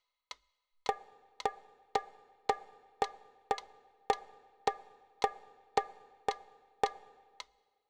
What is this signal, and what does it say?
There is no code.